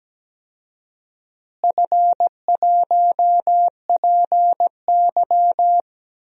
Morse code, heard as a keyed tone, about 17 wpm, 700 Hz. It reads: F1PY
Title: DXCC-100 Call Signs - 4 Characters - Repeated 3x - 17wpm